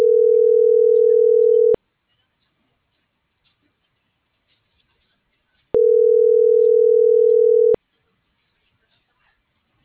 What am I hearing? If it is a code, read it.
Ringback tone (North America), 440 + 480 Hz.